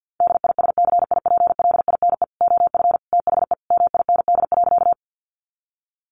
Morse code, heard as a keyed tone, about 50 wpm, 700 Hz. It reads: DISCIPLINE OF THE MIND.
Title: Sentences from Top 900 Words 50wpm